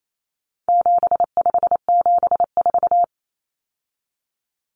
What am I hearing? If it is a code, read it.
Morse code, heard as a keyed tone, about 28 wpm, 700 Hz. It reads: 7574
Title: Common Cut Numbers - 4-Digit 28wpm